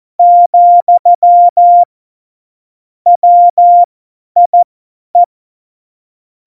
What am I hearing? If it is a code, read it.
Morse code, heard as a keyed tone, about 14 wpm, spaced as for 10 wpm, 700 Hz. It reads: , WIE
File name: Morsekurs / Witze / Witz 20 March 2018, 50 F70